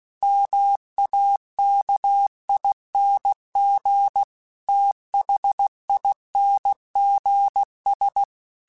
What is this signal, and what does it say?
Morse code, heard as a keyed tone, about 16 wpm, 800 Hz. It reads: MAKING THINGS
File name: Technical Subjects / Technical Subjects-mp3s / AND Gate @15